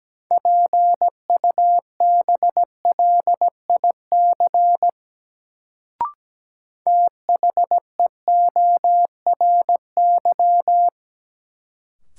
Morse code, heard as a keyed tone, about 17 wpm, 700 Hz. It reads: PUBLIC THEORY
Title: Sentences from Top 600 Words - New Word Drill - 17wpm